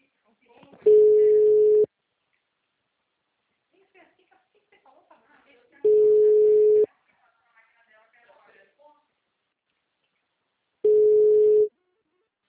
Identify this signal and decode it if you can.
Ringback tone (Europe, 425 Hz), with speech.
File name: out- -2003- - - .4500